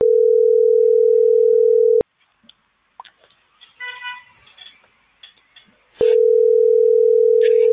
Ringback tone (North America), 440 + 480 Hz.